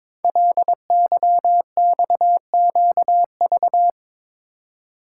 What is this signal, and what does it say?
Morse code, heard as a keyed tone, about 22 wpm, 700 Hz. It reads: LYXQV
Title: Code Groups - 5 Letters - 22wpm